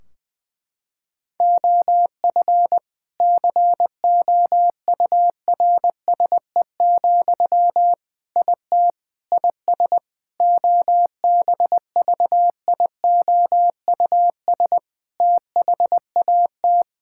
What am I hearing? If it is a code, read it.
Morse code, heard as a keyed tone, about 20 wpm, 700 Hz. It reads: OF COURSE, IT IS OBVIOUS THAT
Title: Sentences from Top 1600 Words 20wpm